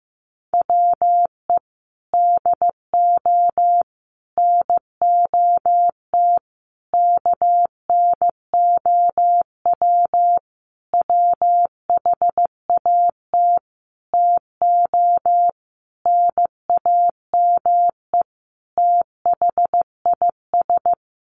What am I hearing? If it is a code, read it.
Morse code, heard as a keyed tone, about 15 wpm, 700 Hz. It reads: WE DO NOT KNOW WHAT TO NAME THIS
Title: Sentences from Top 1500 Words 15wpm